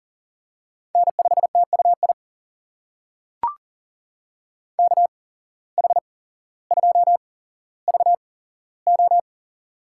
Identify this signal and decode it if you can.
Morse code, heard as a keyed tone, about 40 wpm, spaced as for 17 wpm, 700 Hz. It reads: N5TUI KH2VO
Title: USA Call Signs - 5 Characters - 8x Character Spacing - 40wpm